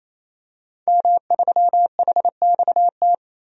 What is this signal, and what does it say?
Morse code, heard as a keyed tone, about 27 wpm, 700 Hz. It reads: M3HXT
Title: DXCC-100 Call Signs - 5 Characters - 28wpm